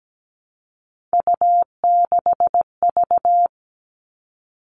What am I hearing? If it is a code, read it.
Morse code, heard as a keyed tone, about 17 wpm, 700 Hz. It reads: U6V
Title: All Cut Numbers - 3-Digit 17wpm